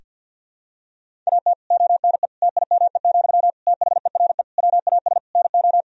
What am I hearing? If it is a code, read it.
Morse code, heard as a keyed tone, about 50 wpm, 700 Hz. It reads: AT ONE TIME, THERE WAS NO
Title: Sentences from Top 1000 Words - Review Part 1 - 50wpm